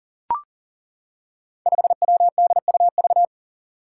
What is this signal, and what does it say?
Morse code, heard as a keyed tone, about 38 wpm, 700 Hz. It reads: 5WDUV
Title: Five Challenging Letter-Number Combinations 40wpm